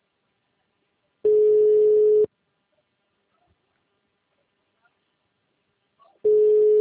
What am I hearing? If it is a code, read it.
Ringback tone (Europe, 425 Hz).